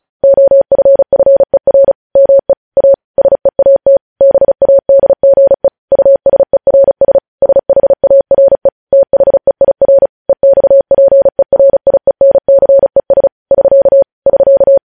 Morse code, heard as a keyed tone, about 36 wpm, 550 Hz. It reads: OFFER ME A SEAT BADGE USERS SHARE THEIR EXPERIENCES <SK> <SK>
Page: news headlines in morse code at 35 wpm